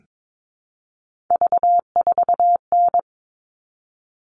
Morse code, heard as a keyed tone, about 22 wpm, 700 Hz. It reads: V4N
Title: All Cut Numbers - 3-Digit 22wpm